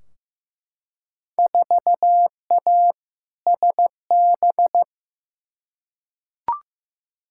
Morse code, heard as a keyed tone, about 15 wpm, 700 Hz. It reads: 4A SB
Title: Field Day Exchanges Encore - 15wpm